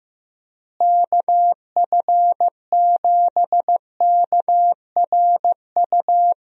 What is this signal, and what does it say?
Morse code, heard as a keyed tone, about 15 wpm, 700 Hz. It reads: KF7KRU
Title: USA Call Signs - 6 Characters - Repeated 3x - 15wpm